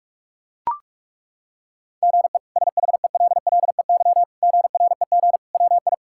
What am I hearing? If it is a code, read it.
Morse code, heard as a keyed tone, about 42 wpm, 700 Hz. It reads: GE SHELLEY GREG WI
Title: K1USN SST QSO Practice - 45wpm